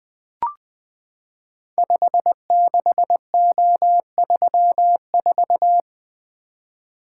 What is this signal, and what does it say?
Morse code, heard as a keyed tone, about 20 wpm, 700 Hz. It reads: 56O34
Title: Code Groups - 5 Letter:Numbers - 20wpm